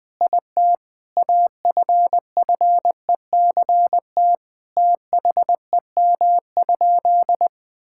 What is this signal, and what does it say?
Morse code, heard as a keyed tone, about 20 wpm, 700 Hz. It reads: IT AFFECT THEM?
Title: Sentences from Top 1000 Words - Review Part 6 - 20wpm